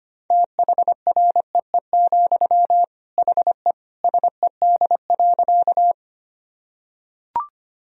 Morse code, heard as a keyed tone, about 25 wpm, 700 Hz. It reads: THREE, HE SED.
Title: Sentences from Top 200 Words and Abbreviations 25wpm